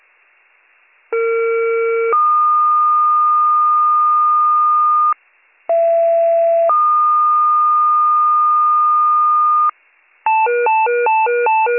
Two-tone page, heard as a one-second tone, then a three-second tone, about 460 then 1220 Hz, about 670 then 1220 Hz. It is heard from 1.1 to 9.7 s.